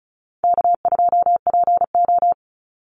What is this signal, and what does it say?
Morse code, heard as a keyed tone, about 34 wpm, 700 Hz. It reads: K2PO
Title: Call Signs Repeated 2x - 35wpm